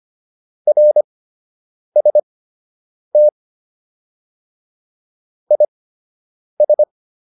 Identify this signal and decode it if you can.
Morse code, heard as a keyed tone, about 25 wpm, spaced as for 8 wpm, 600 Hz. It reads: RST IS